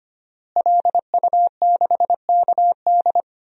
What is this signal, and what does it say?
Morse code, heard as a keyed tone, about 25 wpm, 700 Hz. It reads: LU6KD
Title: DXCC-100 Call Signs - 5 Characters - Repeated 2x - 25wpm